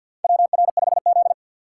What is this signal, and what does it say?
Morse code, heard as a keyed tone, about 48 wpm, 700 Hz. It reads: WR5Z
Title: USA Call Signs - 3x Character Spacing - 50wpm